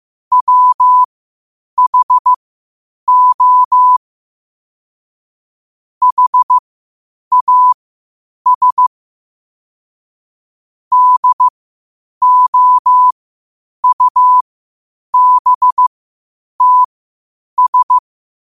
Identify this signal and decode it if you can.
Morse code, heard as a keyed tone, about 15 wpm, spaced as for 8 wpm, 1000 Hz. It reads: WHO HAS DOUBTS
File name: qotdfiles / 8wpm